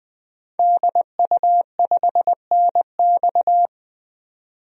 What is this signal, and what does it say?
Morse code, heard as a keyed tone, about 20 wpm, 700 Hz. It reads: DU5NX
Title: DXCC-100 Call Signs - 5 Characters - 20wpm